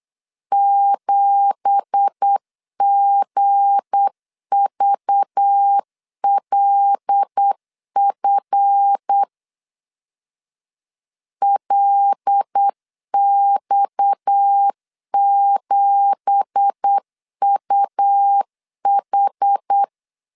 Morse code, heard as a keyed tone, about 8 wpm, 800 Hz. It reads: 7GVLF LX7UH
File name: 8wpm